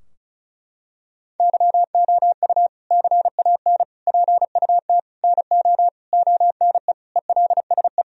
Morse code, heard as a keyed tone, about 34 wpm, 700 Hz. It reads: YOU CAN PUT NO ONE ELSE
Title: Sentences from Top 1000 Words - Review Part 2 - 35wpm